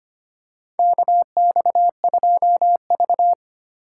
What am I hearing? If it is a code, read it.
Morse code, heard as a keyed tone, about 25 wpm, 700 Hz. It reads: KX2V
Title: USA Call Signs - 4 Characters - 3x Character Spacing - 25wpm